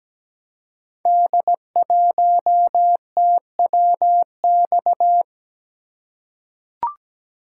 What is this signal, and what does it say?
Morse code, heard as a keyed tone, about 17 wpm, 700 Hz. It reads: D1TWX
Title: Code Groups - 5 Letter:Numbers - 17wpm